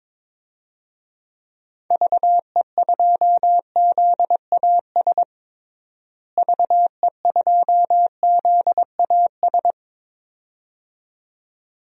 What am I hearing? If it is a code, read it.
Morse code, heard as a keyed tone, about 22 wpm, 700 Hz. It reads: VE2ZAS VE2ZAS
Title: DXCC-100 Call Signs - 6 Characters - Repeated 2x - 22wpm